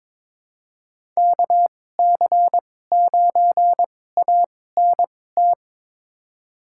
Morse code, heard as a keyed tone, about 22 wpm, spaced as for 16 wpm, 700 Hz. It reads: KC9ANT
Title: USA Call Signs - 6 Characters - 2x Character Spacing - 22wpm